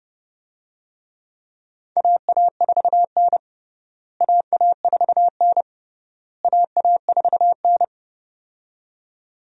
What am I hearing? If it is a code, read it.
Morse code, heard as a keyed tone, about 32 wpm, 700 Hz. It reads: AA4N AA4N AA4N
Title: USA Call Signs - 4 Characters - Repeated 3x - 30wpm